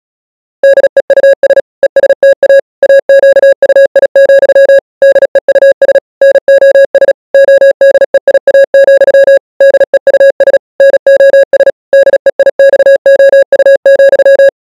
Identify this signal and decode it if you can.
Morse code, heard as a keyed tone, about 37 wpm, 550 Hz. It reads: DEUS ESTA AQUI, DEUS NOS ODEIA, DEUS NOS DEIXOU,